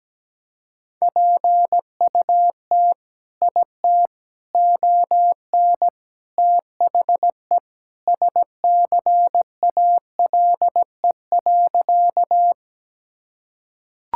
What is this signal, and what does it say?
Morse code, heard as a keyed tone, about 17 wpm, 700 Hz. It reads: PUT IT ON THE SCALE.
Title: Sentences from Top 1000 Words - Review Part 5 - 17wpm